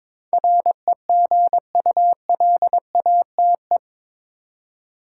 Morse code, heard as a keyed tone, about 21 wpm, 700 Hz. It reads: REGULATE
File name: practice / 22wpm / Eight Letter Words - 22wpm